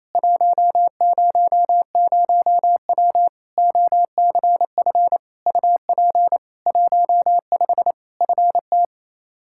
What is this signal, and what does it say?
Morse code, heard as a keyed tone, about 28 wpm, 700 Hz. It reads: 100W OCF UP 15 FT
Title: K1USN SST QSO Practice 28wpm